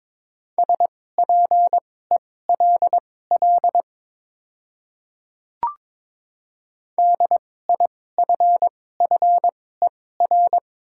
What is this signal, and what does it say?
Morse code, heard as a keyed tone, about 22 wpm, spaced as for 16 wpm, 700 Hz. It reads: SPELL DIFFER
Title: 2x Character Spacing - Top 300 Words - 22wpm